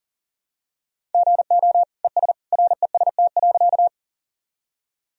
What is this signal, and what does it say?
Morse code, heard as a keyed tone, about 40 wpm, 700 Hz. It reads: GO ES REST.
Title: Sentences from Top 500 Words and Abbreviations 40wpm